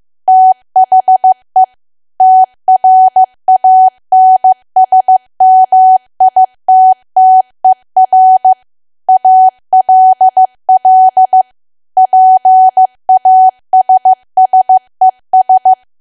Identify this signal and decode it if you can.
Morse code, heard as a keyed tone, about 15 wpm, 750 Hz. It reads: THE TRANSMITTER ALL PASSES
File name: cwa / W1AW-Level2 / 15WPM